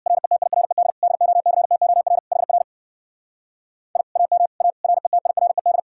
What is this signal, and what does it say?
Morse code, heard as a keyed tone, about 58 wpm, 700 Hz. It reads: RETIRED DOCTOR HR I AM A RETIRED
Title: Speed-Racing Common QSO Phrases - 50wpm